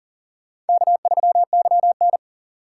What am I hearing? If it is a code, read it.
Morse code, heard as a keyed tone, about 40 wpm, 700 Hz. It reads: K3YN
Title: USA Call Signs - 3x Character Spacing - 40wpm